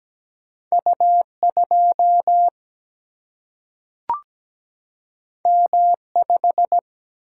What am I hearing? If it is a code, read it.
Morse code, heard as a keyed tone, about 17 wpm, 700 Hz. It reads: U2 M5
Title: Single-Letter Single-Number 17wpm